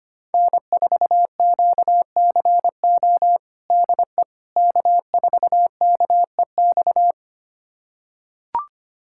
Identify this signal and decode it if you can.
Morse code, heard as a keyed tone, about 25 wpm, 700 Hz. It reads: N4QCO DE K4KEX
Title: USA Callsign DE Callsign 25wpm